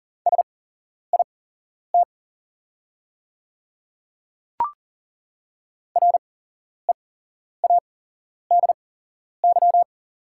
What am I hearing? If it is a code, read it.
Morse code, heard as a keyed tone, about 37 wpm, spaced as for 11 wpm, 690 Hz. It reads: SIT READY
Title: 8x Character Spacing - Top 500 Words - 40wpm